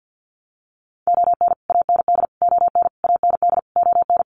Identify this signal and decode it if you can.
Morse code, heard as a keyed tone, about 52 wpm, 700 Hz. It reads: ON AND ON AND ON
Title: Sets of 5 Words Encore2 50wpm